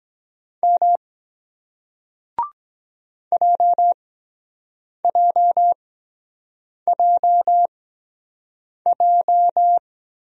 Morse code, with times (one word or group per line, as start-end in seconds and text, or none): 0.6-1.0 M
3.3-9.8 JJJJ